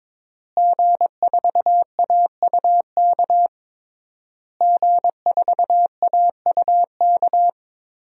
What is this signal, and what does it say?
Morse code, heard as a keyed tone, about 21 wpm, 700 Hz. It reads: G4AUK G4AUK
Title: DXCC-100 Call Signs - 5 Characters - Repeated 3x - 22wpm